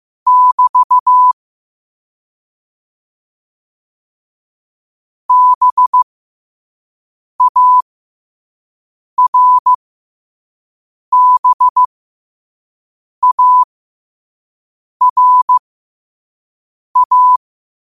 Morse code, heard as a keyed tone, about 15 wpm, spaced as for 5 wpm, 1000 Hz. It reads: <BT> BARBARA
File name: qotdfiles / 5wpm